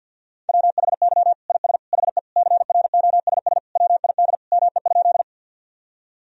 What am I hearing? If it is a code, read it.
Morse code, heard as a keyed tone, about 52 wpm, 700 Hz. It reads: WHY IS HE CROSS WID ME?
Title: Sentences from Top 300 Words and Abbreviations 50wpm